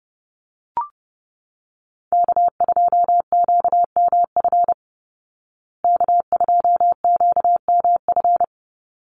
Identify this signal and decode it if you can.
Morse code, heard as a keyed tone, about 30 wpm, 700 Hz. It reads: K2QMF K2QMF
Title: Call Signs Repeated 2x - 30wpm